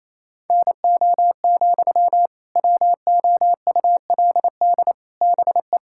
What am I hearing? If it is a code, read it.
Morse code, heard as a keyed tone, about 28 wpm, 700 Hz. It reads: NO, WOULD BE
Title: Sentences from Top 1600 Words 28wpm